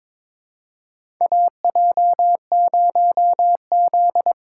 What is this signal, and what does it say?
Morse code, heard as a keyed tone, about 22 wpm, 700 Hz. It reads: AJ0Z
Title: USA Call Signs - 4 Characters - 3x Character Spacing - 22wpm